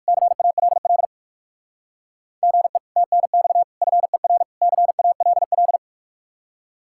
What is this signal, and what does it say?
Morse code, heard as a keyed tone, about 58 wpm, 700 Hz. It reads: CALL GE TNX FER CALL